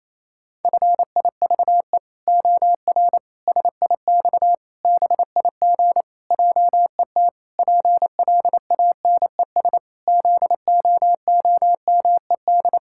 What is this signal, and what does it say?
Morse code, heard as a keyed tone, about 28 wpm, 700 Hz. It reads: FIVE OR SIX BIG JET PLANES ZOOMED
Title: Pangram Challenge v2 - 28wpm